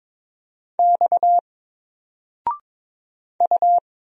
Morse code, heard as a keyed tone, about 22 wpm, 700 Hz. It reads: X U
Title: Single Letter-Number 22wpm